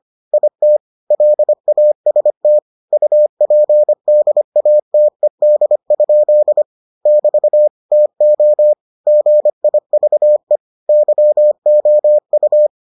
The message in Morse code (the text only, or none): IT LAST UPDATED? <BT> TO GIVE YOU